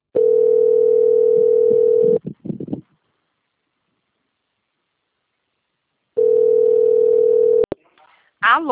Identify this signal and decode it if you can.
Ringback tone (North America), 440 + 480 Hz, with speech.
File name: out- - - - - .49191